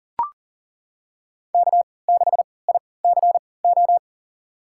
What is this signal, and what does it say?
Morse code, heard as a keyed tone, about 40 wpm, spaced as for 23 wpm, 700 Hz. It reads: K6ICO
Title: USA Call Signs - 5 Characters - 3x Character Spacing - 40wpm